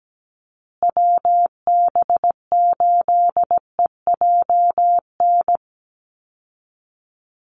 Morse code, heard as a keyed tone, about 17 wpm, 700 Hz. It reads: WB8EJN
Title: Call Signs Repeated 3x - 17wpm